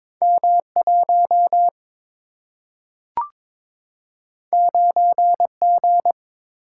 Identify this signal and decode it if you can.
Morse code, heard as a keyed tone, about 22 wpm, 700 Hz. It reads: M1 9G
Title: Single-Letter Single-Number 22wpm